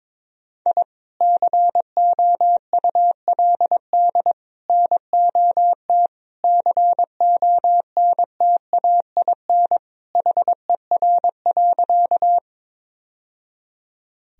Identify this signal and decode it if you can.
Morse code, heard as a keyed tone, about 22 wpm, 700 Hz. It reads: I COULD NOT CONTAIN HER.
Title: Sentences from Top 1000 Words - Review Part 2 - 22wpm